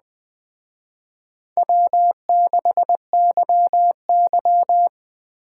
Morse code, heard as a keyed tone, about 19 wpm, 700 Hz. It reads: W6YY